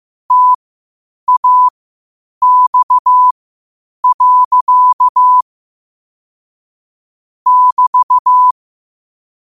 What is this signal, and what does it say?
Morse code, heard as a keyed tone, about 15 wpm, spaced as for 8 wpm, 1000 Hz. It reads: TAX. <BT>
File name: qotdfiles / 8wpm